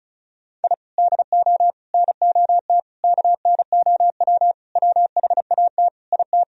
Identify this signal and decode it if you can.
Morse code, heard as a keyed tone, about 36 wpm, 700 Hz. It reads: I DO NOT KNOW WHAT IT
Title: Sentences from Top 1000 Words - Review Part 2 - 35wpm